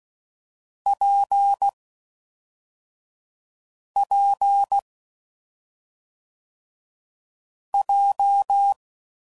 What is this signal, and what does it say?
Morse code, heard as a keyed tone, about 16 wpm, 800 Hz. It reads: P P J
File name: Combinations / Difficult Combos / WPJ WPJ1